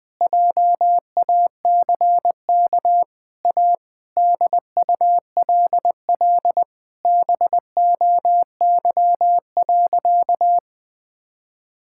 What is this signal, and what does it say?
Morse code, heard as a keyed tone, about 21 wpm, 700 Hz. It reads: JACK A DULL BOY.
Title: Sentences with 8-Word Phrases - Spoken Beginning - 20wpm